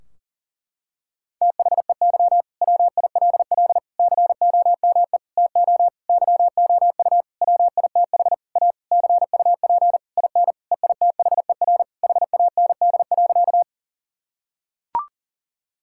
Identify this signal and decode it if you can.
Morse code, heard as a keyed tone, about 40 wpm, 700 Hz. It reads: THEY WILL COME TO YOU WITH A CUP IN EITHER HAND.
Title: Sentences from Top 1600 Words 40wpm